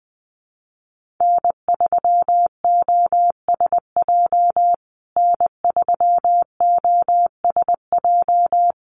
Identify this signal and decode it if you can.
Morse code, heard as a keyed tone, about 20 wpm, 700 Hz. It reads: N3OSJ N3OSJ
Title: USA Call Signs Repeated 2x 20wpm